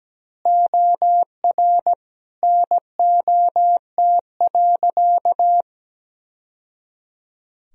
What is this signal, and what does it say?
Morse code, heard as a keyed tone, about 17 wpm, 700 Hz. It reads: OR NOT.
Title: Sentences from Top 1000 Words - Review Part 4 - 17wpm